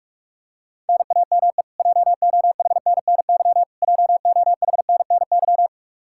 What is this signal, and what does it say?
Morse code, heard as a keyed tone, about 47 wpm, 700 Hz. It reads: NAME JOHNNY JOHNNY
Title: Practice QSO - Exchange 1 - Beginning - 45wpm